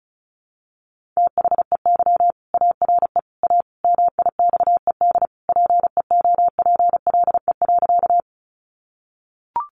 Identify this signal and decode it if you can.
Morse code, heard as a keyed tone, about 34 wpm, 700 Hz. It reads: THEY ARE A MIXED PEOPLE.
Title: Sentences from Top 1500 Words 35wpm